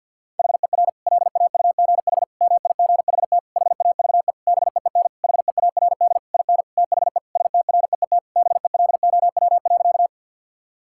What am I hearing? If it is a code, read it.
Morse code, heard as a keyed tone, about 50 wpm, 700 Hz. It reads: HER LAUGH MIGHT HAVE BEEN HEARD IN THE STREET BELOW.
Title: Sentences from Top 1000 Words - Review Part 5 - 50wpm